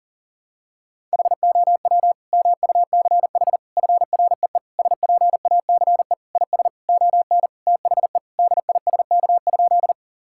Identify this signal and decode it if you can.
Morse code, heard as a keyed tone, about 37 wpm, 700 Hz. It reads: HOW MUCH FREE SPACE IS ON THE DISK?
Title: Sentences from Top 1000 Words - Review Part 6 - 40wpm